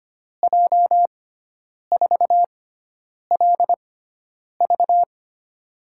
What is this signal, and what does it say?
Morse code, heard as a keyed tone, about 25 wpm, spaced as for 9 wpm, 700 Hz. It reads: J4LV
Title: USA Call Signs - 6x Character Spacing - 25wpm